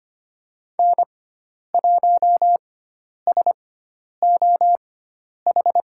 Morse code, heard as a keyed tone, about 25 wpm, spaced as for 10 wpm, 700 Hz. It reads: N1SOH
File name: Lesson 16g - Letter c - Call Signs 25wpm